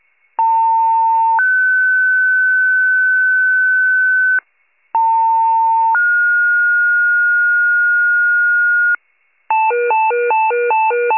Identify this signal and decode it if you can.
Two-tone page, heard as a one-second tone, then a three-second tone, about 900 then 1500 Hz, about 900 then 1430 Hz. It is heard from 0.4 to 9.0 s.
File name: pager / 78 FIRE / wcalert